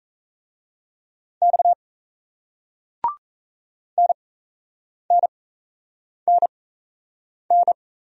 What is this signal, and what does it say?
Morse code, heard as a keyed tone, about 38 wpm, spaced as for 8 wpm, 700 Hz. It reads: X NNNN